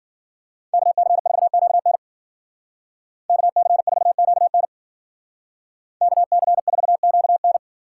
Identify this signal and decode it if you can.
Morse code, heard as a keyed tone, about 48 wpm, 700 Hz. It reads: KC4QN KC4QN KC4QN
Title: Speed Racing - USA Call Signs 40wpm